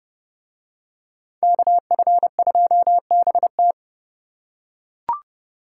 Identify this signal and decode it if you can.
Morse code, heard as a keyed tone, about 29 wpm, 700 Hz. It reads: KF2BT